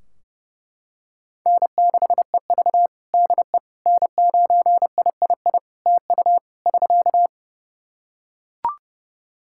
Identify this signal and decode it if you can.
Morse code, heard as a keyed tone, about 32 wpm, 700 Hz. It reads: N6EV DE N9III TU <SK>
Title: Practice QSO - Exchange 4 - Ending - 30wpm